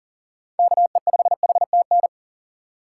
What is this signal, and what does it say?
Morse code, heard as a keyed tone, about 40 wpm, 700 Hz. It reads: KE5HTN
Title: USA Call Signs - 4x Character Spacing - 40wpm